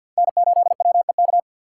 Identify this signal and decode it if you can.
Morse code, heard as a keyed tone, about 53 wpm, 700 Hz. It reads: N8WEK
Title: USA Call Signs - 2x Character Spacing - 50wpm